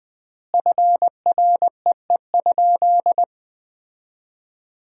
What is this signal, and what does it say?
Morse code, heard as a keyed tone, about 20 wpm, 700 Hz. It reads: FREE?
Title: 4x Word-Spacing - Sentences from Top 500 Words - 20wpm